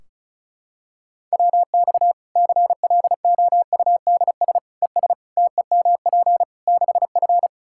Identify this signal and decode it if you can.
Morse code, heard as a keyed tone, about 36 wpm, 700 Hz. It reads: WX CLOUDS ES TEMP 6F
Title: Practice QSO - Exchange 2 - Info - 35wpm